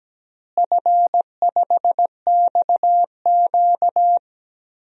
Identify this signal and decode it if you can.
Morse code, heard as a keyed tone, about 17 wpm, 700 Hz. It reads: F5XQ
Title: DXCC-100 Call Signs - 4 Characters - Repeated 3x - 17wpm